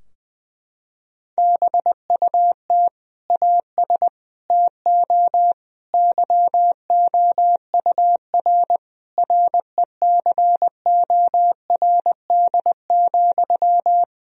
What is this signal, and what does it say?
Morse code, heard as a keyed tone, about 20 wpm, 700 Hz. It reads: BUT AS TO YOUR RECORD,
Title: Sentences from Top 1000 Words - Review Part 3 - 20wpm